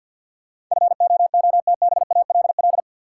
Morse code, heard as a keyed tone, about 52 wpm, 700 Hz. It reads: FOOTBALL
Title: Sentences from Top 1600 Words - New Word Drill - 50wpm